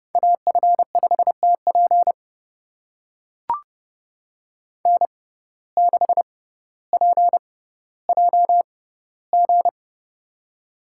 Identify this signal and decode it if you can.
Morse code, heard as a keyed tone, about 30 wpm, spaced as for 15 wpm, 700 Hz. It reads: AF5TP N6PJG
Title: USA Call Signs - 5 Characters - 6x Character Spacing - 30wpm